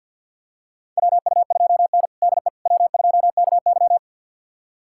Morse code, heard as a keyed tone, about 47 wpm, 700 Hz. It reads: WU2N DE W2KY
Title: USA Callsign DE Callsign - 4 Characters - 50wpm